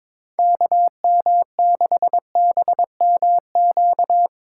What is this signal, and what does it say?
Morse code, heard as a keyed tone, about 22 wpm, 700 Hz. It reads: KM6BMQ
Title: USA Call Signs - 6 Characters - 3x Character Spacing - 22wpm